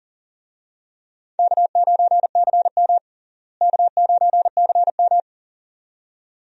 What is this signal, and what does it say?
Morse code, heard as a keyed tone, about 40 wpm, 700 Hz. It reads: K9CM K9CM
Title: USA Call Signs - 4 Characters - Repeated 2x - 40wpm